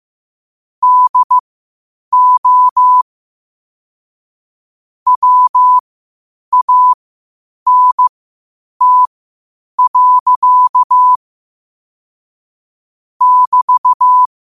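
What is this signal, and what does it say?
Morse code, heard as a keyed tone, about 15 wpm, spaced as for 8 wpm, 1000 Hz. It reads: DO WANT. <BT>
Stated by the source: Morse code at 8 wpm.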